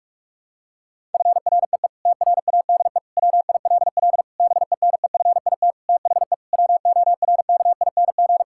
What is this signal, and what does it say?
Morse code, heard as a keyed tone, about 41 wpm, 700 Hz. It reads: FREE TRADE WILL BENEFIT THE WORKING
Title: Sentences from Top 1000 Words - Review Part 6 - 45wpm